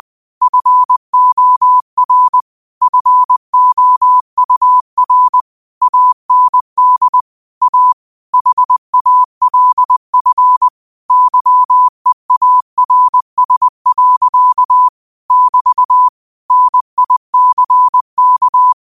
Morse code, heard as a keyed tone, about 20 wpm, 1000 Hz. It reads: FOR FOUR AND A HALF YEARS. <BT> NICK
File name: qotdfiles / 20wpm